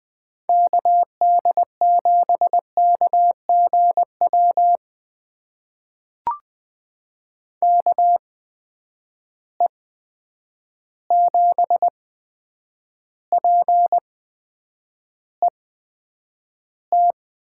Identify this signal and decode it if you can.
Morse code, heard as a keyed tone, about 20 wpm, spaced as for 9 wpm, 700 Hz. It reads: KD7KGW KE7PET
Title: USA Call Signs - 6 Characters - 8x Character Spacing - 20wpm